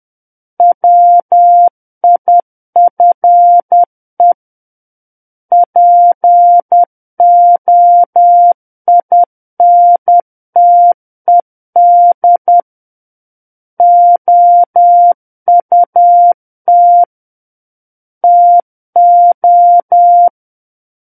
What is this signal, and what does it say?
Morse code, heard as a keyed tone, about 10 wpm, 700 Hz. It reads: WIFE POINTED OUT TO
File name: Book One / 01-Chapter 1 - 10 WPM